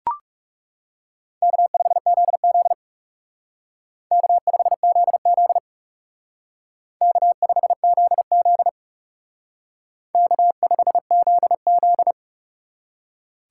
Morse code, heard as a keyed tone, about 36 wpm, 700 Hz. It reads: K5ZZ K5ZZ K5ZZ K5ZZ
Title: Speed Racing - USA 1x2 Call Signs 30wpm